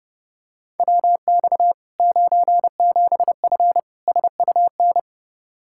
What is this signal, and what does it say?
Morse code, heard as a keyed tone, about 30 wpm, 700 Hz. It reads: WX 97F SUN
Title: Practice QSO - Exchange 2 - Info - 30wpm